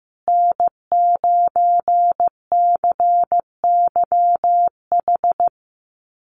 Morse code, heard as a keyed tone, about 15 wpm, 700 Hz. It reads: N9CYH
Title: USA Call Signs Repeated 3x 15wpm